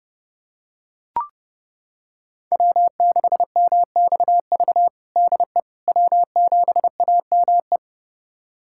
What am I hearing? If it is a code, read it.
Morse code, heard as a keyed tone, about 30 wpm, 700 Hz. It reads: W6MXV DE W7AME
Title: USA Callsign DE Callsign - 5 Characters - 30wpm